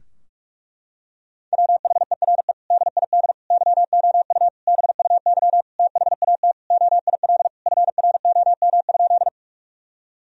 Morse code, heard as a keyed tone, about 41 wpm, 700 Hz. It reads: WHERE DID YOU BUY THAT OIL FROM?
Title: Sentences from Top 1000 Words - Review Part 4 - 45wpm